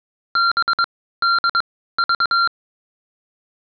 Morse code, heard as a keyed tone, about 22 wpm, spaced as for 15 wpm, 1380 Hz. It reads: BDV